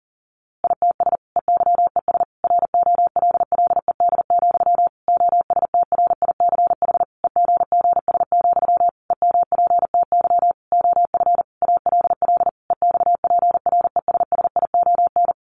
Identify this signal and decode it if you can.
Morse code, heard as a keyed tone, about 40 wpm, 700 Hz. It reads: ITS EYES ROLLED, OSTRICH EGGS, EMPTY OF ALL EXPRESSION
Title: A Sound Of Thunder 40wpm